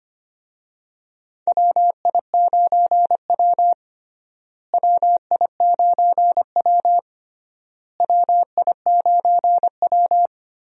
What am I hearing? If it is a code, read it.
Morse code, heard as a keyed tone, about 25 wpm, 700 Hz. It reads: WI9W WI9W WI9W
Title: USA Call Signs - 4 Characters - Repeated 3x - 25wpm